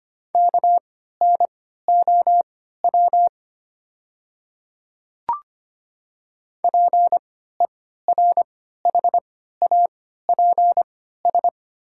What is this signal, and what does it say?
Morse code, heard as a keyed tone, about 25 wpm, spaced as for 14 wpm, 690 Hz. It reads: KNOW PERHAPS
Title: 3x Character Spacing - Top 500 Words - 25wpm